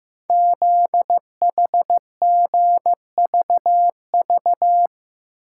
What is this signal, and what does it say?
Morse code, heard as a keyed tone, about 15 wpm, 700 Hz. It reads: ZHGVV
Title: Code Groups - 5 Letters - 15wpm